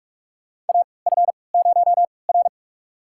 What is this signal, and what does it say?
Morse code, heard as a keyed tone, about 44 wpm, spaced as for 25 wpm, 700 Hz. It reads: AF0R
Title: USA Call Signs - 4 Characters - 3x Character Spacing - 45wpm